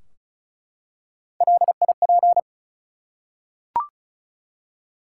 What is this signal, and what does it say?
Morse code, heard as a keyed tone, about 36 wpm, 700 Hz. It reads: LIP
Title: Three Letter Words - 35wpm